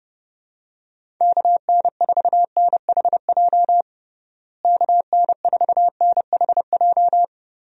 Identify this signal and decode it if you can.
Morse code, heard as a keyed tone, about 30 wpm, 700 Hz. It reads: KN4NHJ KN4NHJ
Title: USA Call Signs - 6 Characters - Repeated 2x - 30wpm